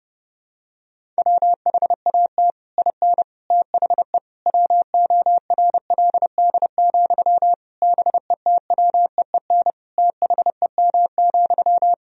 Morse code, heard as a keyed tone, about 30 wpm, 700 Hz. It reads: WHAT IN THE WORLD, BETWEEN THEM,
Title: Sentences from Top 1000 Words - Review Part 1 - 30wpm